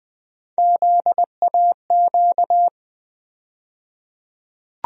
Morse code, heard as a keyed tone, about 20 wpm, 700 Hz. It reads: ZAQ
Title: USA Call Sign Postfixes - 3 Letters 20wpm